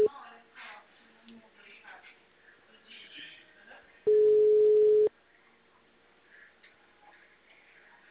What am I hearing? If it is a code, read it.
Ringback tone (Europe, 425 Hz), with speech.
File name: out- -2003- - - .99332